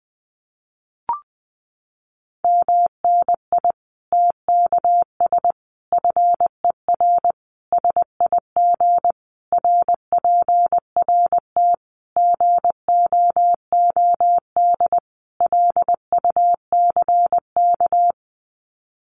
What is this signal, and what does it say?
Morse code, heard as a keyed tone, about 20 wpm, 700 Hz. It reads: MNI TKS FER SIG RPRT GOOD LUCK
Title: Common QSO Phrases 20wpm